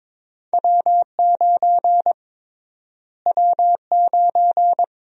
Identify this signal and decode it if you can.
Morse code, heard as a keyed tone, about 22 wpm, 700 Hz. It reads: W9 W9
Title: USA Call Sign Prefixes - Repeated 2x - 22wpm